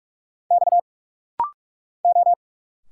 Morse code, heard as a keyed tone, about 45 wpm, 700 Hz. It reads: X O
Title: Single Letter-Number Alt - Rapid-Fire 45wpm